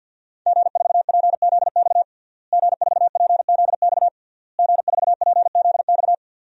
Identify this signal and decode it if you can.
Morse code, heard as a keyed tone, about 52 wpm, 700 Hz. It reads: G4PZX G4PZX G4PZX
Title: DXCC-100 Call Signs - 5 Characters - Repeated 3x - 50wpm